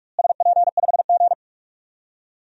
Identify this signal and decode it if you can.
Morse code, heard as a keyed tone, about 41 wpm, 700 Hz. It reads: SP5G
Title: DXCC-100 Call Signs - 4 Characters - Repeated 3x - 45wpm